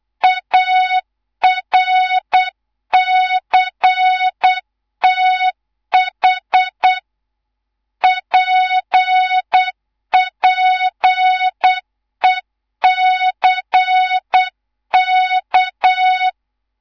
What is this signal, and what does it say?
Morse code, heard as a keyed tone, about 8 wpm, 3700 Hz, with noise. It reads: ARCTH PPECK
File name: Morze-HA2MN / hang / 11 GYAKORLAT